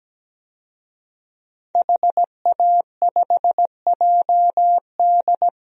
Morse code, heard as a keyed tone, about 17 wpm, 700 Hz. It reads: HA5JD
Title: DXCC-100 Call Signs - 5 Characters - 17wpm